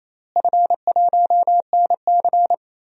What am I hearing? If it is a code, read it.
Morse code, heard as a keyed tone, about 28 wpm, 700 Hz. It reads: F1NC
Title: DXCC-100 Call Signs - 4 Characters - Repeated 2x - 28wpm